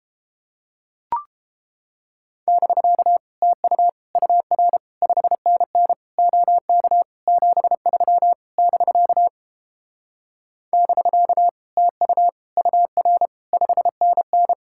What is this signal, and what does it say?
Morse code, heard as a keyed tone, about 29 wpm, 700 Hz. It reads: <BK> TU UR 5NN OK 73 <BK> <BK> TU UR 5NN